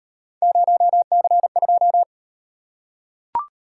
Morse code, heard as a keyed tone, about 37 wpm, 700 Hz. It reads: 0C2